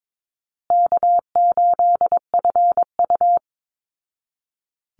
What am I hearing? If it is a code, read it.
Morse code, heard as a keyed tone, about 22 wpm, 700 Hz. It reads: K8FU
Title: Easy Call Signs Repeated 3x - 22wpm